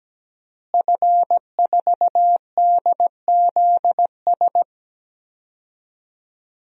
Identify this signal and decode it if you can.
Morse code, heard as a keyed tone, about 17 wpm, 700 Hz. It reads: F4DZS
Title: DXCC-100 Call Signs - 5 Characters - Repeated 3x - 17wpm